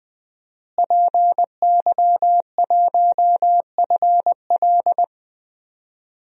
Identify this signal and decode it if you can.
Morse code, heard as a keyed tone, about 20 wpm, 700 Hz. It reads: PY1FL
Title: DXCC-100 Call Signs - 5 Characters - Repeated 2x - 20wpm